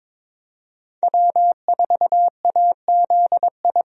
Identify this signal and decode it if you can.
Morse code, heard as a keyed tone, about 21 wpm, 700 Hz. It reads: W4AZI
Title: USA Call Signs - 5 Characters - Repeated 3x - 22wpm